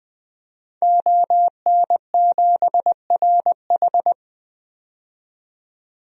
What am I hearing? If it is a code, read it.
Morse code, heard as a keyed tone, about 20 wpm, 700 Hz. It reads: ON7RH
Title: DXCC-100 Call Signs - 5 Characters - 20wpm